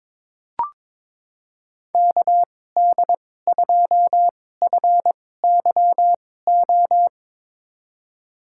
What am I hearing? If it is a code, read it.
Morse code, heard as a keyed tone, about 22 wpm, spaced as for 16 wpm, 700 Hz. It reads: KD2FYO
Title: USA Call Signs - 6 Characters - 2x Character Spacing - 22wpm